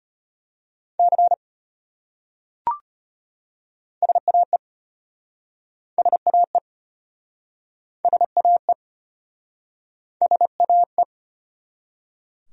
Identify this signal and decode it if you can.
Morse code, heard as a keyed tone, about 33 wpm, 700 Hz. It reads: C SAE SAE SAE SAE